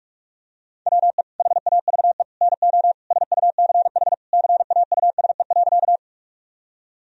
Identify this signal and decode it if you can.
Morse code, heard as a keyed tone, about 46 wpm, 700 Hz. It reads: WE HAVE NO SUCH CAUSE.
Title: Sentences from Top 1000 Words - Review Part 1 - 45wpm